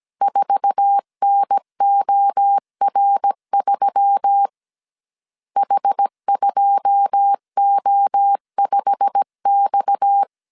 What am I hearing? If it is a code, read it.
Morse code, heard as a keyed tone, about 17 wpm, 800 Hz. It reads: 4NOR3 H2O5X
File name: CW / DET / 1 9 1500C S A 16wpm 2828